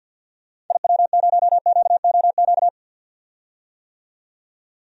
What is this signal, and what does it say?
Morse code, heard as a keyed tone, about 48 wpm, 700 Hz. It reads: IW0QOQ
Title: DXCC-100 Call Signs - 6 Characters - 50wpm